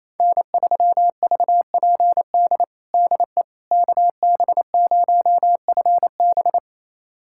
Morse code, heard as a keyed tone, about 28 wpm, 700 Hz. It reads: N3VPD DE KB0FB
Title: USA Callsign DE Callsign - 5 Characters - 28wpm